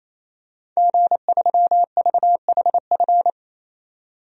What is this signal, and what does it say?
Morse code, heard as a keyed tone, about 28 wpm, 700 Hz. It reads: G3VHF
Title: DXCC-100 Call Signs - 5 Characters - 28wpm